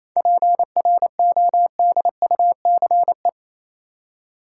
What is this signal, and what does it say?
Morse code, heard as a keyed tone, about 28 wpm, 700 Hz. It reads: PRODUCE
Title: Seven Letter Words - 28wpm